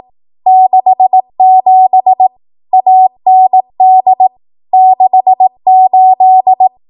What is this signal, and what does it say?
Morse code, heard as a keyed tone, about 18 wpm, 750 Hz. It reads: 67 AND 68
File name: Archive / 18 WPM / 18WPM